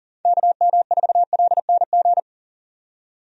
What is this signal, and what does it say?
Morse code, heard as a keyed tone, about 40 wpm, 700 Hz. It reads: KM4LNG
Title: USA Call Signs - 6 Characters - 40wpm